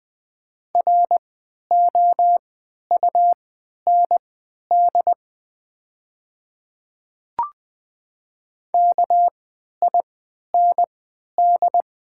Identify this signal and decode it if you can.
Morse code, heard as a keyed tone, about 20 wpm, spaced as for 11 wpm, 700 Hz. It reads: ROUND KIND
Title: 3x Character Spacing - Top 200 Words - 20wpm